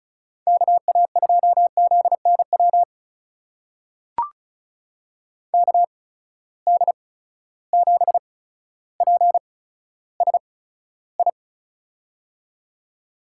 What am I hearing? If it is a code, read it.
Morse code, heard as a keyed tone, about 36 wpm, spaced as for 15 wpm, 700 Hz. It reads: KA2ZNW KD7PSI
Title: USA Call Signs - 6 Characters - 8x Character Spacing - 35wpm